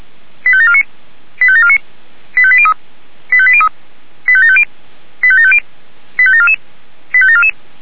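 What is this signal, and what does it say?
Five-tone selective call (ZVEI), heard as quick runs of five tones, tones of about 70 ms. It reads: 85749 85749 85803 85803 85750 85750 85744 85744
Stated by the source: Five-tone ZVEI selective call received on an SDR.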